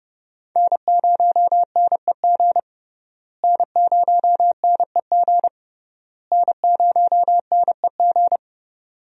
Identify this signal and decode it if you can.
Morse code, heard as a keyed tone, about 32 wpm, 700 Hz. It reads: N0NEG N0NEG N0NEG
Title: USA Call Signs - 5 Characters - Repeated 3x - 30wpm